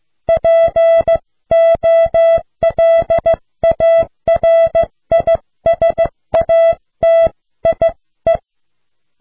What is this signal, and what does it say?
Morse code, heard as a keyed tone, about 16 wpm, 660 Hz. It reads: POLARISATIE